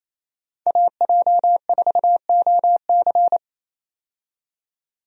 Morse code, heard as a keyed tone, about 28 wpm, 700 Hz. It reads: AJ4OC
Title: USA Call Signs - 5 Characters - 28wpm